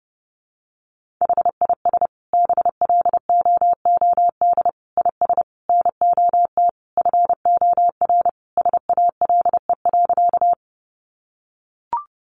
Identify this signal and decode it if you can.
Morse code, heard as a keyed tone, about 30 wpm, 700 Hz. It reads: HIS BLOOD IS NOT FOR SALE.
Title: Sentences from Top 1400 Words 30wpm